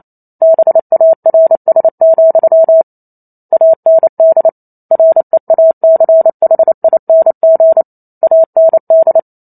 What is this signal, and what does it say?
Morse code, heard as a keyed tone, about 28 wpm, 650 Hz. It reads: BARS, AND REACHING AND